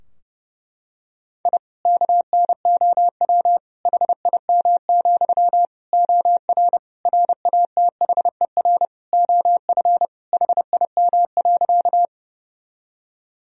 Morse code, heard as a keyed tone, about 30 wpm, 700 Hz. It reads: I KNOW HIM, OR RATHER OF HIM.
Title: Sentences from Top 1000 Words - Review Part 4 - 30wpm